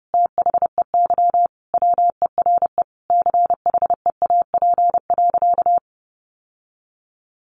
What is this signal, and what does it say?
Morse code, heard as a keyed tone, about 30 wpm, 700 Hz. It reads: THEY WERE CHEAP.
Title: Sentences from Top 1200 Words 30wpm